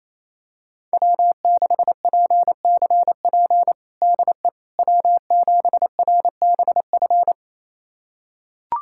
Morse code, heard as a keyed tone, about 28 wpm, 700 Hz. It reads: W6PCP DE W7RBF
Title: USA Callsign DE Callsign - 5 Characters - 28wpm